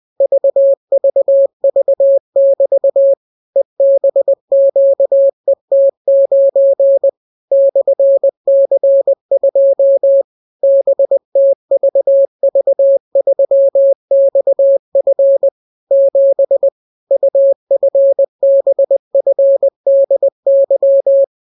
Morse code, heard as a keyed tone, about 20 wpm, 550 Hz. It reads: VVV<BT> EBQET9 /C2 BTVV3XF 7 UFBFDY